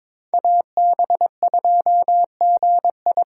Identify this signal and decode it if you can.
Morse code, heard as a keyed tone, about 22 wpm, 700 Hz. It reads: AB2GI
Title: K1USN SST QSO Practice - 22wpm